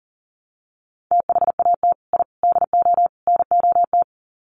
Morse code, heard as a keyed tone, about 40 wpm, 700 Hz. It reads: THAT I DO NOT
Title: Sets of 4 Words 40wpm